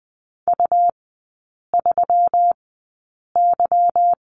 Morse code, heard as a keyed tone, about 20 wpm, spaced as for 8 wpm, 690 Hz. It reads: U3Y
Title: 145 Lesson 24e - Letter y - Three Characters 20wpm